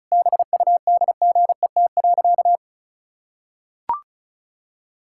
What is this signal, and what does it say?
Morse code, heard as a keyed tone, about 34 wpm, 700 Hz. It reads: BUDGET.
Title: Pangram Challenge v2 - 35wpm